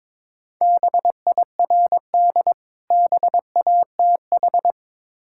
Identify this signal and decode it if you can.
Morse code, heard as a keyed tone, about 23 wpm, 690 Hz. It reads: BIRD BATH
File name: Difficult Words with Hints - 22wpm